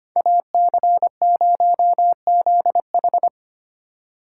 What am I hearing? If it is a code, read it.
Morse code, heard as a keyed tone, about 29 wpm, 700 Hz. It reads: AC0ZH
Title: Speed Racing - USA 2x2 Call Signs 25wpm